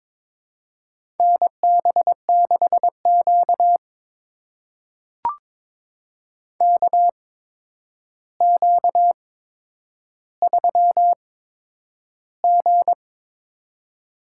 Morse code, heard as a keyed tone, about 23 wpm, spaced as for 10 wpm, 700 Hz. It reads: NB6Q KQ3G
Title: USA Call Signs - 4 Characters - 8x Character Spacing - 22wpm